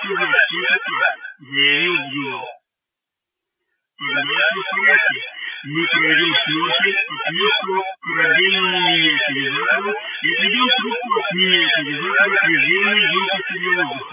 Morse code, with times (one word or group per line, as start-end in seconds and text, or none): none